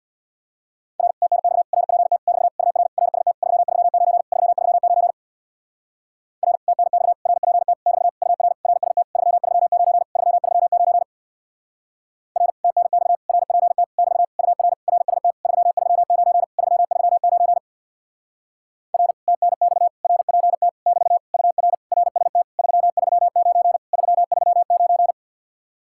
Morse code, heard as a keyed tone, about 62 wpm, 700 Hz. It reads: R TNX RPT <BT> UR RST 339 339 R TNX RPT <BT> UR RST 339 339 R TNX RPT <BT> UR RST 339 339 R TNX RPT <BT> UR RST 339 339
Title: Speed-Racing Common QSO Phrases Encore - 50wpm